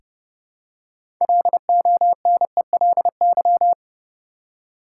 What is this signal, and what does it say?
Morse code, heard as a keyed tone, about 31 wpm, 700 Hz. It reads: LONELY
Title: Six Letter Words - 30wpm